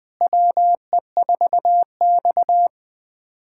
Morse code, heard as a keyed tone, about 20 wpm, 700 Hz. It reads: WE4X
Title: USA Callsign DE Callsign - 4 Characters - 20wpm